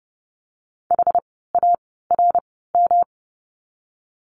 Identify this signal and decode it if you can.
Morse code, heard as a keyed tone, about 31 wpm, spaced as for 17 wpm, 700 Hz. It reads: HARM
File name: course / 30:10wpm / 108 Lesson 18f - Letter m - Words 30wpm